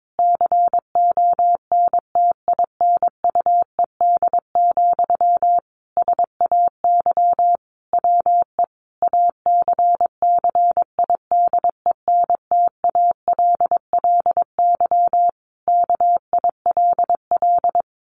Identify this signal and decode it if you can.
Morse code, heard as a keyed tone, about 22 wpm, 700 Hz. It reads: CONTINUED, SAY WE ACCIDENTALLY KILL
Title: A Sound Of Thunder 22wpm